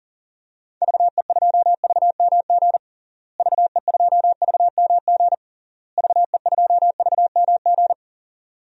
Morse code, heard as a keyed tone, about 40 wpm, 700 Hz. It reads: VE2VMG VE2VMG VE2VMG
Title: DXCC-100 Call Signs - 6 Characters - Repeated 3x - 40wpm